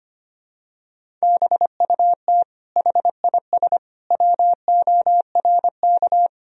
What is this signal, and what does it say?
Morse code, heard as a keyed tone, about 25 wpm, 700 Hz. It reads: BUT HIS WORK
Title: Sentences from Top 1000 Words - Review Part 2 - 25wpm